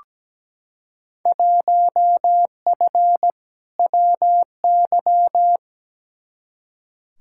Morse code, heard as a keyed tone, about 17 wpm, 700 Hz. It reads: 1F WY
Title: Field Day Exchanges Encore - 17wpm